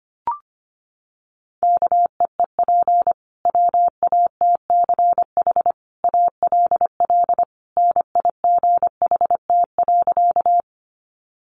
Morse code, heard as a keyed tone, about 25 wpm, 700 Hz. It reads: KEEP WATCH ALL NIGHT.
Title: Sentences from Top 300 Words 25wpm